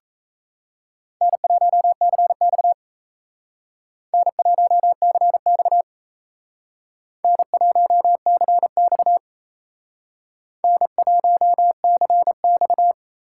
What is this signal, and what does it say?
Morse code, heard as a keyed tone, about 37 wpm, 700 Hz. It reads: N1CX N1CX N1CX N1CX